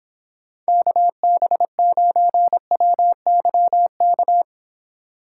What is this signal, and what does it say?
Morse code, heard as a keyed tone, about 26 wpm, 700 Hz. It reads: KB9WYK